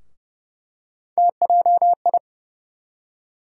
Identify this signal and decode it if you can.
Morse code, heard as a keyed tone, about 30 wpm, 700 Hz. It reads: TJI
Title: USA Call Sign Postfixes - 3 Letters 30wpm